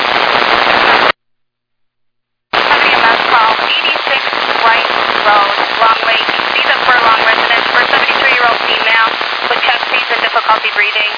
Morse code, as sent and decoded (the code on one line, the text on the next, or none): none